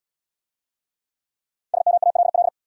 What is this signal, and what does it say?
Morse code, heard as a keyed tone, about 64 wpm, 700 Hz. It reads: SMALL